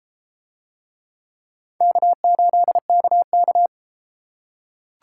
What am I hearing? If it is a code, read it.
Morse code, heard as a keyed tone, about 30 wpm, 700 Hz. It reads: K8KK